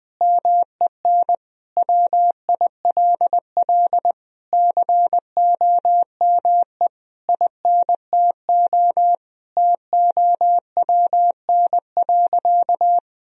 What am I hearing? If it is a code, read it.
Morse code, heard as a keyed tone, about 20 wpm, 700 Hz. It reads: MEN WILL COME INTO TOWN.
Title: Sentences from Top 1000 Words - Review Part 2 - 20wpm